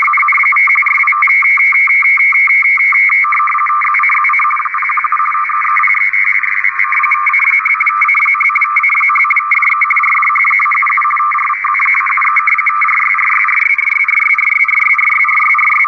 Two-tone teleprinter signal, about 850 Hz shift.
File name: europe / rus / rus-4260-arkhangelsk-meteo-rtty-50 bd-(1995)-(rb)-(dat-917-14-xc)